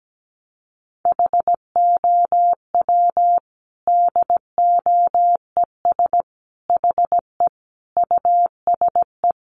Morse code, heard as a keyed tone, about 17 wpm, 700 Hz. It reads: HOW DOES HE USE
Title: Sentences from Top 200 Words 17wpm